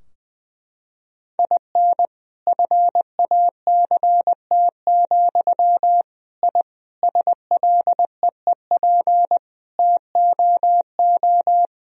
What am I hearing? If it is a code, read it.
Morse code, heard as a keyed tone, about 20 wpm, 700 Hz. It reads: IN FACT, I SLEEP TOO
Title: Sentences from Top 1000 Words - Review Part 3 - 20wpm